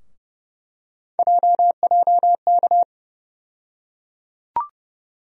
Morse code, heard as a keyed tone, about 32 wpm, 700 Hz. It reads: JJK
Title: USA Call Sign Postfixes - 3 Letters 30wpm